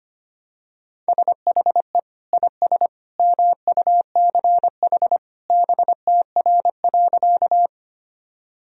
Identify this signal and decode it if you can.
Morse code, heard as a keyed tone, about 25 wpm, 700 Hz. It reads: SHE IS MUCH BTR.
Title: Sentences from Top 400 Words and Abbreviations 25wpm